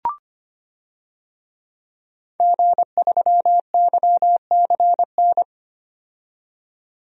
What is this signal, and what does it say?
Morse code, heard as a keyed tone, about 25 wpm, 700 Hz. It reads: G3YCN
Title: DXCC-100 Call Signs - 5 Characters - 25wpm